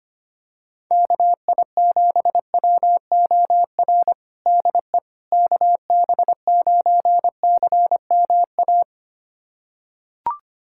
Morse code, heard as a keyed tone, about 25 wpm, 700 Hz. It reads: KI7WOR DE KB9CMA
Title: USA Callsign DE Callsign - 6 Characters - 25wpm